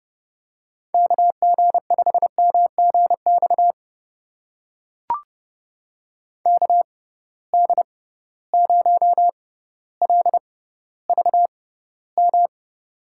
Morse code, heard as a keyed tone, about 30 wpm, spaced as for 15 wpm, 700 Hz. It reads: KG5MGX KD0LVM
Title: USA Call Signs - 6 Characters - 6x Character Spacing - 30wpm